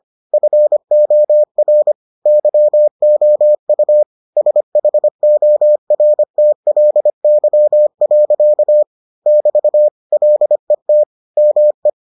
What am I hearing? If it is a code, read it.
Morse code, heard as a keyed tone, about 25 wpm, 600 Hz. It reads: FOR YOU SHORTLY. <BT> LET ME